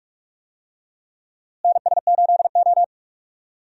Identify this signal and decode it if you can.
Morse code, heard as a keyed tone, about 46 wpm, 700 Hz. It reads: NS8O
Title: USA Call Signs - 4 Characters - Repeated 2x - 45wpm